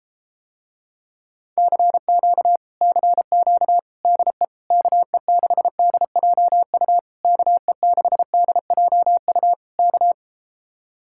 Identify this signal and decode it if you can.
Morse code, heard as a keyed tone, about 33 wpm, 700 Hz. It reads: CQ CQ DE KE6DJU KE6DJU K